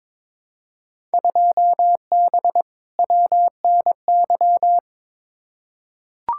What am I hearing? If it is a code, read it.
Morse code, heard as a keyed tone, about 22 wpm, 700 Hz. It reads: 2B WNY
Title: Field Day Exchanges Encore - 22wpm